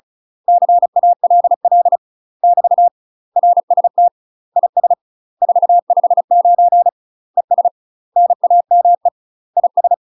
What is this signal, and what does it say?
Morse code, heard as a keyed tone, about 36 wpm, 700 Hz. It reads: CALL <BT> RST IS 459 ES NAME IS
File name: UK MASTER QSOs TRAINING / 35WPM / UK MASTER QSOs 0035